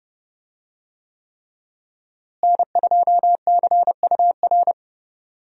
Morse code, heard as a keyed tone, about 30 wpm, 700 Hz. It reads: N2CUR
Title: USA Call Signs - 5 Characters - Repeated 3x - 30wpm